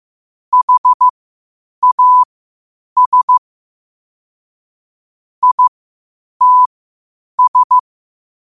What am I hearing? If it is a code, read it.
Morse code, heard as a keyed tone, about 15 wpm, spaced as for 8 wpm, 1000 Hz. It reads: HAS ITS
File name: qotdfiles / 8wpm